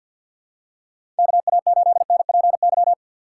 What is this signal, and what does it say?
Morse code, heard as a keyed tone, about 53 wpm, 700 Hz. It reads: KA8NPY
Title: USA Call Signs - 6 Characters - 6x Character Spacing - 50wpm